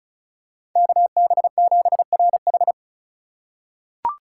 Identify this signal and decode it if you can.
Morse code, heard as a keyed tone, about 36 wpm, 700 Hz. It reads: KB7RH
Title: USA Call Signs - 5 Characters - 4x Character Spacing - 35wpm